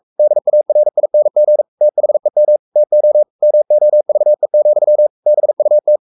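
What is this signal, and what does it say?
Morse code, heard as a keyed tone, about 44 wpm, 600 Hz. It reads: DARING THEM TO MOVE, BUT
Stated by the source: Morse code at 43 wpm sending Ken Follett's novel Never in English, from CW med Gnister.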